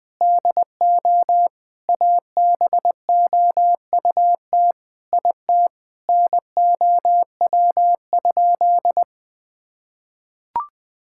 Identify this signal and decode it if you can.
Morse code, heard as a keyed tone, about 20 wpm, 700 Hz. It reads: DO ABOUT IT NOW?
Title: Sentences with 7-Word Phrases - 20wpm